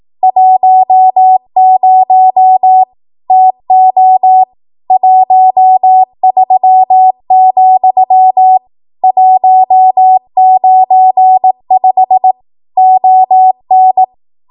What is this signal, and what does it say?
Morse code, heard as a keyed tone, about 18 wpm, 750 Hz. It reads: 10 TO 13, 195 ON
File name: Archive / 18 WPM / 18WPM